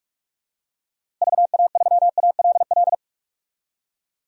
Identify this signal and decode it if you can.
Morse code, heard as a keyed tone, about 41 wpm, 700 Hz. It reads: VA3ALL
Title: DXCC-100 Call Signs - 6 Characters - Repeated 2x - 45wpm